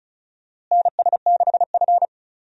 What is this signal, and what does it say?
Morse code, heard as a keyed tone, about 36 wpm, 700 Hz. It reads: NS6F
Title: USA Call Signs - 4 Characters - Repeated 3x - 35wpm